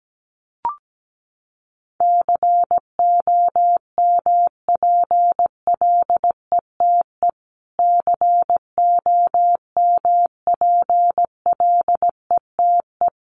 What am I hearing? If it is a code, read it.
Morse code, heard as a keyed tone, about 17 wpm, 700 Hz. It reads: COMPLETE COMPLETE
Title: Top 500 Words Repeated 17wpm